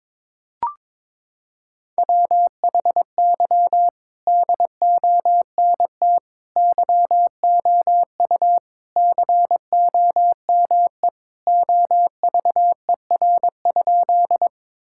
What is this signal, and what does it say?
Morse code, heard as a keyed tone, about 22 wpm, 700 Hz. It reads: WHY DONT YOU COME OVER?
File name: practice / 22wpm / Sentences with 4-Word Phrases - 22wpm